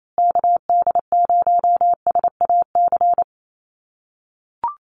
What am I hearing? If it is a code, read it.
Morse code, heard as a keyed tone, about 28 wpm, 700 Hz. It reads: KD0SAC
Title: USA Call Signs Repeated 2x 28wpm